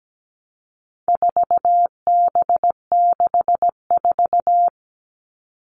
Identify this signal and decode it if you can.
Morse code, heard as a keyed tone, about 17 wpm, 700 Hz. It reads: 4B64
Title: All Cut Numbers - 4-Digit 17wpm